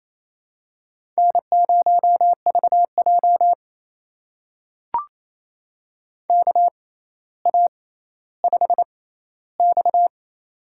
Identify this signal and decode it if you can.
Morse code, heard as a keyed tone, about 28 wpm, spaced as for 14 wpm, 700 Hz. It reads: N0VJ KA5X
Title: USA Call Signs - 4 Characters - 6x Character Spacing - 28wpm